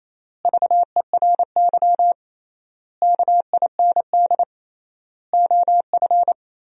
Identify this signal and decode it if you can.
Morse code, heard as a keyed tone, about 27 wpm, 700 Hz. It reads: VERY KIND OF
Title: 3x Word-Spacing - Sentences from Top 300 Words - 28wpm